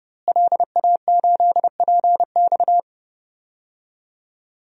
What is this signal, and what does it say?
Morse code, heard as a keyed tone, about 30 wpm, 700 Hz. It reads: LA8PX
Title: DXCC-100 Call Signs - 5 Characters - Repeated 3x - 30wpm